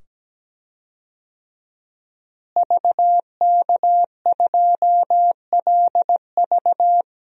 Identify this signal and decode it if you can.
Morse code, heard as a keyed tone, about 17 wpm, 700 Hz. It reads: VK2LV
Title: DXCC-100 Call Signs - 5 Characters - Repeated 2x - 17wpm